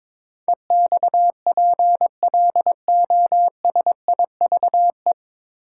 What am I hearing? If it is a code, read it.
Morse code, heard as a keyed tone, about 22 wpm, 700 Hz. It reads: EXPLOSIVE
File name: 9 Letter Words - First 3 Letters Given - 22wpm